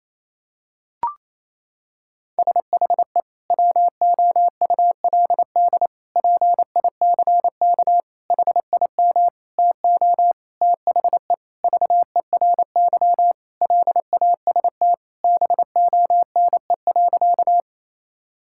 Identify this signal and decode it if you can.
Morse code, heard as a keyed tone, about 28 wpm, 700 Hz. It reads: SHE WOULD PICK HIM TO THE VERY LAST BONE.
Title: Sentences from Top 1600 Words 28wpm